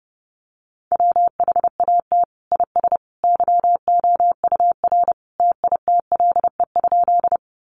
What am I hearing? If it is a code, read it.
Morse code, heard as a keyed tone, about 30 wpm, 700 Hz. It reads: WHAT IS YOUR TITLE?
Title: Sentences from Top 1200 Words 30wpm